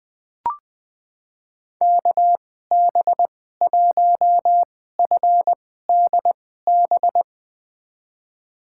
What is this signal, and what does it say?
Morse code, heard as a keyed tone, about 20 wpm, spaced as for 14 wpm, 700 Hz. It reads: KB1FDB
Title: USA Call Signs - 2x Character Spacing - 20wpm